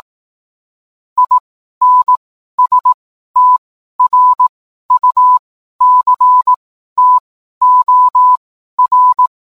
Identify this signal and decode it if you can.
Morse code, heard as a keyed tone, about 18 wpm, spaced as for 12 wpm, 1000 Hz. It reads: INSTRUCTOR